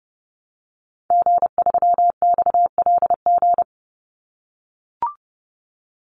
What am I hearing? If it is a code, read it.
Morse code, heard as a keyed tone, about 30 wpm, 700 Hz. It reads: G3XLG
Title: Call Signs - 30wpm